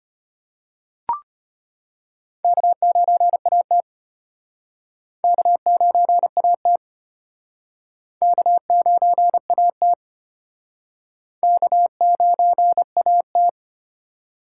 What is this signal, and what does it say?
Morse code, heard as a keyed tone, about 34 wpm, 700 Hz. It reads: K9AT K9AT K9AT K9AT